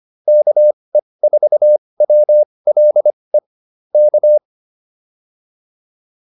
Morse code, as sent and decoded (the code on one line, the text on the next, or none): -.- . ....- .-- .-.. / -.-
KE4WLE K